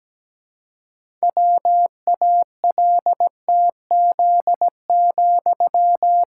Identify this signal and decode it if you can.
Morse code, heard as a keyed tone, about 17 wpm, 700 Hz. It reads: WALTZ,
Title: Pangram Challenge v2 - 17wpm